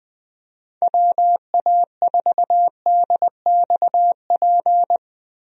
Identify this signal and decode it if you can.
Morse code, heard as a keyed tone, about 20 wpm, 700 Hz. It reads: WA4DXP
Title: USA Call Signs - 6 Characters - Repeated 3x - 20wpm